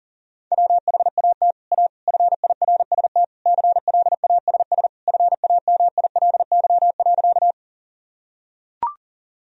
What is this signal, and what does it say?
Morse code, heard as a keyed tone, about 40 wpm, 700 Hz. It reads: WHAT A FIRST CLASS FAMILY.
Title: Sentences from Top 1000 Words - Review Part 2 - 40wpm